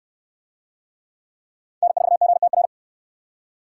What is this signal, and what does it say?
Morse code, heard as a keyed tone, about 69 wpm, 700 Hz. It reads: N4GTR